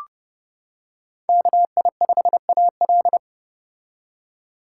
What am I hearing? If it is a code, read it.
Morse code, heard as a keyed tone, about 31 wpm, 700 Hz. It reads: KI5AL
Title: K1USN SST QSO Practice - 30wpm